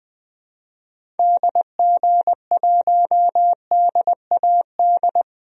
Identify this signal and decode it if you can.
Morse code, heard as a keyed tone, about 21 wpm, 700 Hz. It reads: DG1DAD
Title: DXCC-100 Call Signs - 6 Characters - Repeated 2x - 20wpm